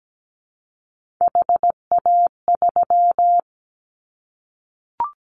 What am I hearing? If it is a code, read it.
Morse code, heard as a keyed tone, about 17 wpm, 700 Hz. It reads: HA3
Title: Callsign Prefixes for Top 25 DXCC - Repeated 3x - 17wpm